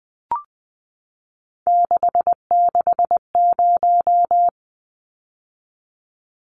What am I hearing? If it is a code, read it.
Morse code, heard as a keyed tone, about 20 wpm, 700 Hz. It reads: 660
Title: Numbers - 3-Digit 20wpm